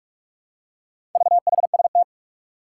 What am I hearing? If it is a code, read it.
Morse code, heard as a keyed tone, about 44 wpm, 700 Hz. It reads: VHST
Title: Four Challenging Letter-Number Combinations 45wpm